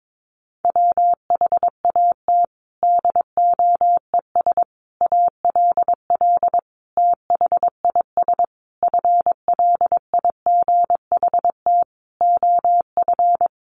Morse code, heard as a keyed tone, about 22 wpm, 700 Hz. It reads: WHAT DOES ALL THIS FLIGHT OF
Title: Sentences from Top 1300 Words 22wpm